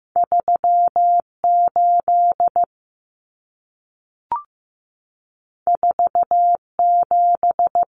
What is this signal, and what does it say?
Morse code, heard as a keyed tone, about 15 wpm, 700 Hz. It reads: 38 47
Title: Numbers - 2-Digit 15wpm